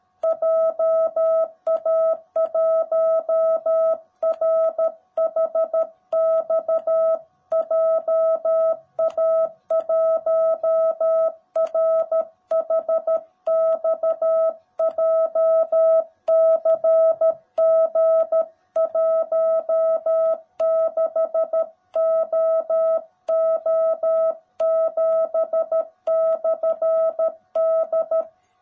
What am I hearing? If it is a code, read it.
Morse code, heard as a keyed tone, about 13 wpm, 650 Hz. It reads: JA1RHX JA1RHX JCG16OO7/D
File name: jh8cbh / 2021cw / cw18b